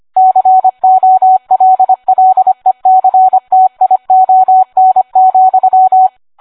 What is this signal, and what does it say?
Morse code, heard as a keyed tone, about 25 wpm, 750 Hz. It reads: COLLECTION,